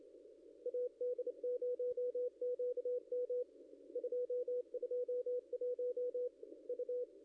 Morse code, heard as a keyed tone, about 28 wpm, 490 Hz, with radio noise. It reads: AD0QM 221 U
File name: contests / ss-cw-k5tr-op-n5rz -0600